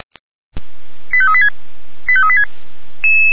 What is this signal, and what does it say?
Five-tone selective call (ZVEI), heard as quick runs of five tones, tones of about 70 ms. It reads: 85386 85386